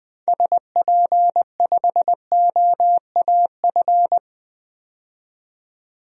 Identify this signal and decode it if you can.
Morse code, heard as a keyed tone, about 20 wpm, 700 Hz. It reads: SP5OAF
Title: DXCC-100 Call Signs - 6 Characters - 20wpm